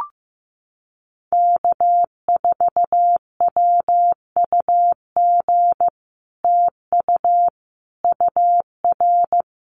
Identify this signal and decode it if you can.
Morse code, heard as a keyed tone, about 15 wpm, 700 Hz. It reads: K4WUG TU UR
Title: Common QSO Phrases Encore 15wpm